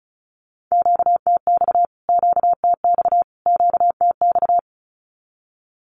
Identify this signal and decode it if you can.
Morse code, heard as a keyed tone, about 34 wpm, 700 Hz. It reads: QTX QTX QTX
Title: Q-Codes Repeated 3x 35wpm